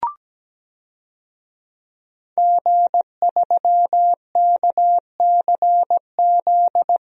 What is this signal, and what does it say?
Morse code, heard as a keyed tone, about 17 wpm, 700 Hz. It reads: G3KCZ
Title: DXCC-100 Call Signs - 5 Characters - Repeated 2x - 17wpm